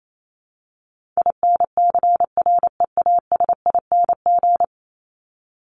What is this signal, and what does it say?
Morse code, heard as a keyed tone, about 28 wpm, 700 Hz. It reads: INCREASING
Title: ING Suffix - Top 100 Words with Base Words 28wpm